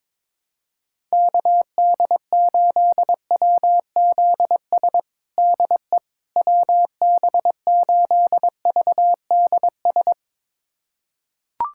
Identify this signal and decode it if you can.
Morse code, heard as a keyed tone, about 22 wpm, 700 Hz. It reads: KD8WZS DE WB8VDS
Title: USA Callsign DE Callsign - 6 Characters - 22wpm